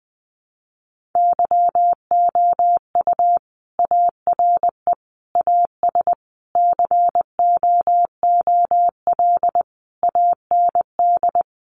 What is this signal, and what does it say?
Morse code, heard as a keyed tone, about 20 wpm, 700 Hz. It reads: YOU ARE AS COOL AND
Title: Sentences from Top 1500 Words 20wpm